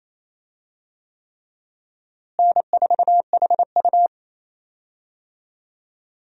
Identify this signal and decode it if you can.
Morse code, heard as a keyed tone, about 28 wpm, 700 Hz. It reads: N4HU
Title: USA Call Signs - 4 Characters - Repeated 3x - 28wpm